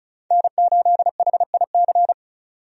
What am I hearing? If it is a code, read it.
Morse code, heard as a keyed tone, about 34 wpm, 700 Hz. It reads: N8HIC
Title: USA Call Signs - 5 Characters - 35wpm